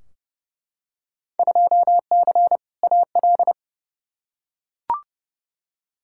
Morse code, heard as a keyed tone, about 30 wpm, 700 Hz. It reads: 2C AL
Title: Field Day Exchanges Encore - 30wpm